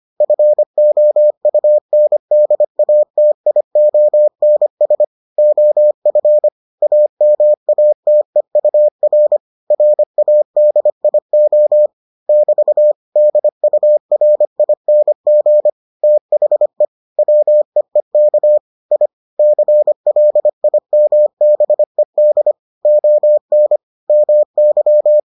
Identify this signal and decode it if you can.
Morse code, heard as a keyed tone, about 25 wpm, 600 Hz. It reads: FOUNDATIONS OF AMATEUR RADIO <BT> DURING THE WEEK I CLIMBED ON MY